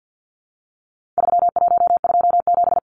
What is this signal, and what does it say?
Morse code, heard as a keyed tone, about 50 wpm, 700 Hz. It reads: 3127
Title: Numbers - 4-Digit 50wpm